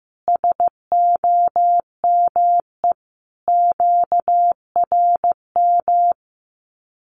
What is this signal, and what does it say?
Morse code, heard as a keyed tone, about 15 wpm, 700 Hz. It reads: SOME QRM
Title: Common QSO Phrases 15wpm